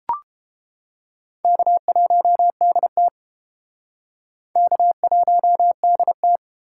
Morse code, heard as a keyed tone, about 31 wpm, 700 Hz. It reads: K1DT K1DT